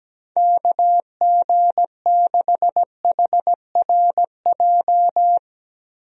Morse code, heard as a keyed tone, about 17 wpm, 700 Hz. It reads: KG6HRJ
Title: USA Call Signs - 6 Characters - Repeated 3x - 17wpm